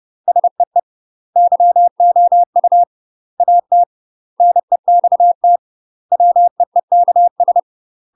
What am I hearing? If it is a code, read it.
Morse code, heard as a keyed tone, about 30 wpm, 700 Hz. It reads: SEE YOU AT NEXT WEEKS
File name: UK MASTER QSOs TRAINING / 30WPM / UK MASTER QSOs 0001